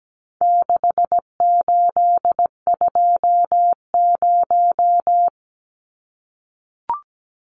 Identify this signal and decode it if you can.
Morse code, heard as a keyed tone, about 17 wpm, 700 Hz. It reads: 6820
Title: Numbers - 4-Digit 17wpm